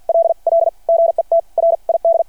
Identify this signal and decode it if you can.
Morse code, heard as a keyed tone, about 53 wpm, spaced as for 43 wpm, 670 Hz. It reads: LL GET RID
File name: archived / CRSnet